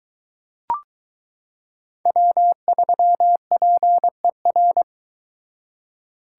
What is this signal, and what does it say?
Morse code, heard as a keyed tone, about 22 wpm, 700 Hz. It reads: W3PER